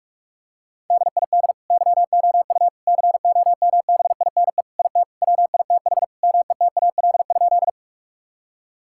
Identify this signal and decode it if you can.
Morse code, heard as a keyed tone, about 46 wpm, 700 Hz. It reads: DID YOU COMBINE IT WITH METAL?
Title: Sentences from Top 1600 Words 45wpm